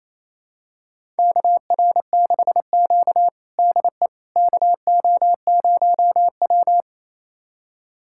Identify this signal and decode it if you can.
Morse code, heard as a keyed tone, about 28 wpm, 700 Hz. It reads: KR6Q DE KO0W
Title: USA Callsign DE Callsign - 4 Characters - 28wpm